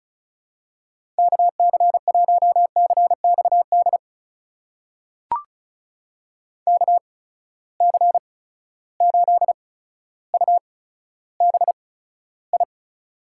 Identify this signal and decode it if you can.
Morse code, heard as a keyed tone, about 36 wpm, spaced as for 15 wpm, 700 Hz. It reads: KC1CXD KC8UBI
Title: USA Call Signs - 6 Characters - 8x Character Spacing - 35wpm